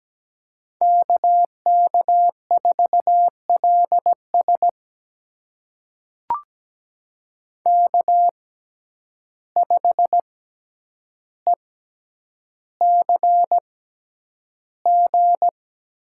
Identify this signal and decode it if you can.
Morse code, heard as a keyed tone, about 17 wpm, spaced as for 9 wpm, 700 Hz. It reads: KK4LS K5ECG
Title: USA Call Signs - 5 Characters - 6x Character Spacing - 17wpm